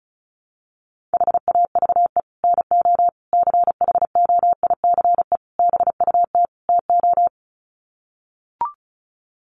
Morse code, heard as a keyed tone, about 34 wpm, 700 Hz. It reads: HAVE NO CHOICE BUT TO
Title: Sets of 5 Words Encore 35wpm